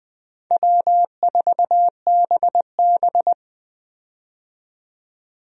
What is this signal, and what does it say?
Morse code, heard as a keyed tone, about 20 wpm, 700 Hz. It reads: W4BB
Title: USA Call Signs - 4 Characters - 20wpm